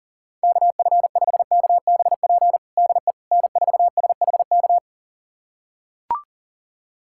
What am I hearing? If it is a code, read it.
Morse code, heard as a keyed tone, about 37 wpm, 700 Hz. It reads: KF5KBP DE N4SHK
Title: USA Callsign DE Callsign 40wpm